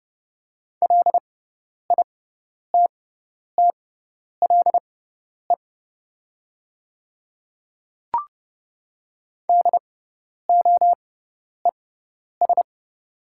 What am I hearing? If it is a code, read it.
Morse code, heard as a keyed tone, about 30 wpm, spaced as for 10 wpm, 700 Hz. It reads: LITTLE DOES
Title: 6x Character Spacing - Top 200 Words - 30wpm